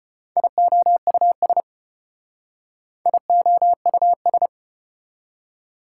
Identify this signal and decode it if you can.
Morse code, heard as a keyed tone, about 32 wpm, 700 Hz. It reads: IOUS IOUS